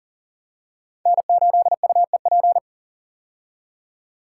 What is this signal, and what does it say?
Morse code, heard as a keyed tone, about 39 wpm, 700 Hz. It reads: N8UEP
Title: Code Groups - 5 Letter:Numbers - 40wpm